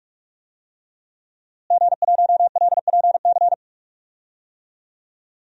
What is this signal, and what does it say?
Morse code, heard as a keyed tone, about 45 wpm, 700 Hz. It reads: G1LPC
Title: DXCC-100 Call Signs - 5 Characters - Repeated 2x - 45wpm